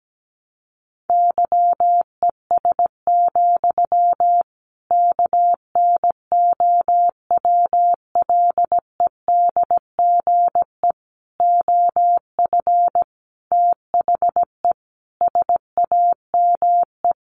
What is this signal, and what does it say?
Morse code, heard as a keyed tone, about 17 wpm, 700 Hz. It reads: YES, KNOWLEDGE OF THE SAME
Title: Sentences from Top 1500 Words 17wpm